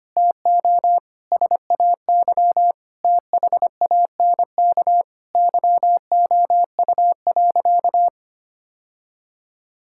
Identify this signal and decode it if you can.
Morse code, heard as a keyed tone, about 25 wpm, 700 Hz. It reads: TO SAY THANK YOU.
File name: practice / 25wpm / Sentences with 2-Word Phrases - 25wpm